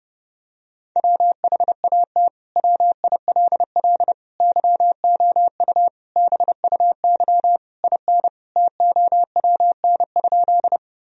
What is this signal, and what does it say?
Morse code, heard as a keyed tone, about 30 wpm, 700 Hz. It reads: WHAT WILL YOU BUY IN TOWN?
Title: Sentences from Top 1000 Words - Review Part 4 - 30wpm